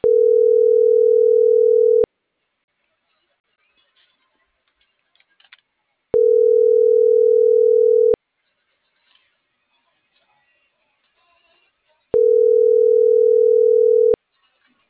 Ringback tone (North America), 440 + 480 Hz.